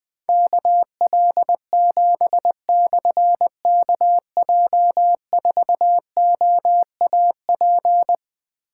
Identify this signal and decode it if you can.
Morse code, heard as a keyed tone, about 20 wpm, 700 Hz. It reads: KL7/KJ4OAP
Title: Difficult Call Signs 20wpm